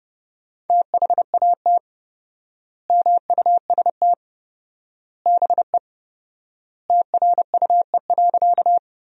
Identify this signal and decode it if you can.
Morse code, heard as a keyed tone, about 30 wpm, 700 Hz. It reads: THAT MUST BE TRUE.
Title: 4x Word-Spacing - Sentences from Top 400 Words - 30wpm